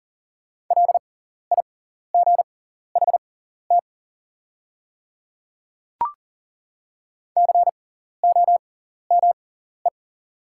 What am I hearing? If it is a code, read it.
Morse code, heard as a keyed tone, about 40 wpm, spaced as for 14 wpm, 700 Hz. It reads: LIGHT COME
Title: 6x Character Spacing - Top 200 Words - 40wpm